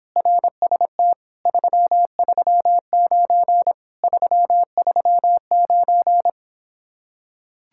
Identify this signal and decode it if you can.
Morse code, heard as a keyed tone, about 26 wpm, 700 Hz. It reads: RST 339 339